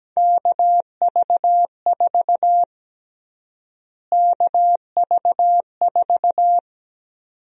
Morse code, heard as a keyed tone, about 17 wpm, 700 Hz. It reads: KV4 KV4
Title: USA Call Sign Prefixes - Repeated 2x - 17wpm